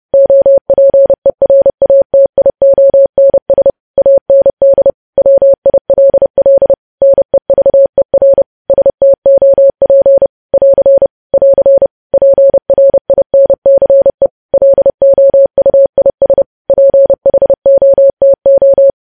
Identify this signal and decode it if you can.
Morse code, heard as a keyed tone, about 30 wpm, 550 Hz. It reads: OPERATIONS AND WILL NEVER STOP <AR> <AR> PRINCE LOUIS PHOTO
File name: podcast / News headlines-1x-30x30